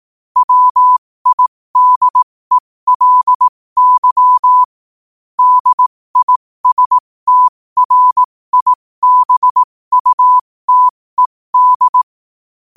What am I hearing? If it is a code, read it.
Morse code, heard as a keyed tone, about 18 wpm, spaced as for 15 wpm, 1000 Hz. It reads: WIDELY DISTRIBUTED
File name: qotdfiles / 15wpm